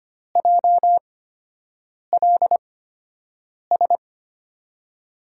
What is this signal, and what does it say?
Morse code, heard as a keyed tone, about 25 wpm, spaced as for 7 wpm, 700 Hz. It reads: JLS
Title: USA Call Signs - 5 Characters - 8x Character Spacing - 25wpm